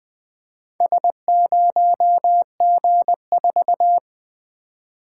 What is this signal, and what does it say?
Morse code, heard as a keyed tone, about 20 wpm, 700 Hz. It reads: S0G4
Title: Four Letter-Number Combinations - 20wpm